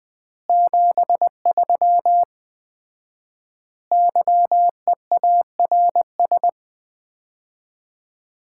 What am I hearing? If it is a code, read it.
Morse code, heard as a keyed tone, about 21 wpm, 700 Hz. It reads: 73 YEARS
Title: Common QSO Phrases - 4x Word-Spacing - 20wpm